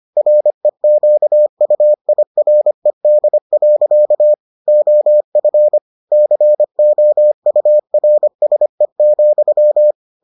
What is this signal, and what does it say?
Morse code, heard as a keyed tone, about 25 wpm, 600 Hz. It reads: REQUIRED. OF COURSE,